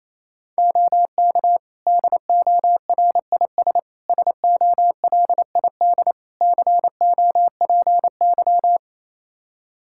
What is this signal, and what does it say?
Morse code, heard as a keyed tone, about 28 wpm, 700 Hz. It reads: OK DORIS SOLID COPY
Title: Practice QSO - Exchange 3 - More Info - 28wpm